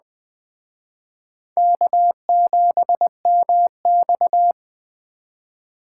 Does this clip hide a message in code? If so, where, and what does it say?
Yes: Morse code from 1.6 to 4.5 s, heard as a keyed tone, about 21 wpm, 700 Hz. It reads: K7MX